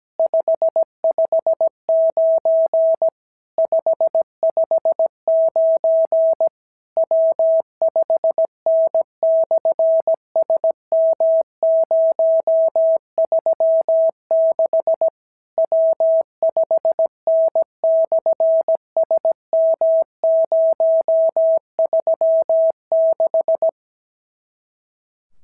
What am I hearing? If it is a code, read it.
Morse code, heard as a keyed tone, about 17 wpm, 650 Hz. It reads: 559 559 W5N/SM036 W5N/SM036
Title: SOTA QSO Practice 17wpm